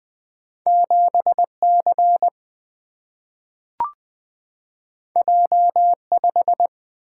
Morse code, heard as a keyed tone, about 20 wpm, 700 Hz. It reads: 7 C J 5
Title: Single-Letter Single-Number 20wpm